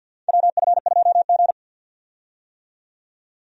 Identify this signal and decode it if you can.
Morse code, heard as a keyed tone, about 53 wpm, 700 Hz. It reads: WF2G
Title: USA Call Signs - 4 Characters - 50wpm